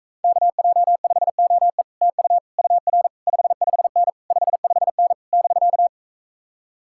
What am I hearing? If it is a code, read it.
Morse code, heard as a keyed tone, about 45 wpm, 700 Hz. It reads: KJ5OE TU UR 55N 55N <BK>